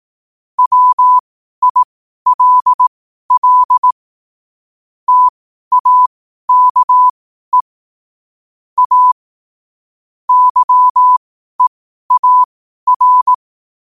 Morse code, heard as a keyed tone, about 18 wpm, spaced as for 12 wpm, 1000 Hz. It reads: WILL TAKE A YEAR
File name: qotdfiles / 12wpm